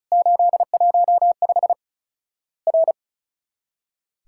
Morse code, heard as a keyed tone, about 36 wpm, 700 Hz. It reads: 815 R
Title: Worked All Europe DX Contest - QTC Exchange - 35wpm